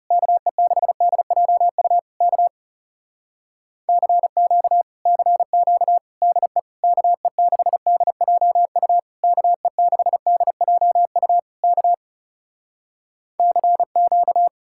Morse code, heard as a keyed tone, about 40 wpm, 700 Hz. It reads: KE6DJU K CQ CQ DE KE6DJU KE6DJU K CQ